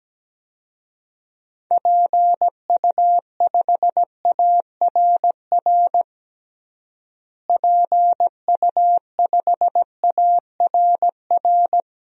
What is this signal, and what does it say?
Morse code, heard as a keyed tone, about 17 wpm, 700 Hz. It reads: PU5ARR PU5ARR
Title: DXCC-100 Call Signs - 6 Characters - Repeated 2x - 17wpm